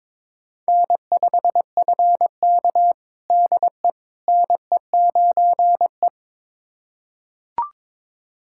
Morse code, heard as a keyed tone, about 22 wpm, 700 Hz. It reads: N5FK DE NE9E
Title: USA Callsign DE Callsign - 4 Characters - 22wpm